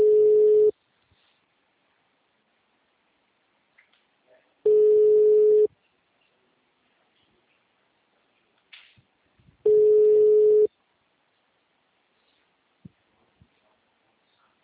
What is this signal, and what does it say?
Ringback tone (Europe, 425 Hz).
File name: out- -2016- - - .16364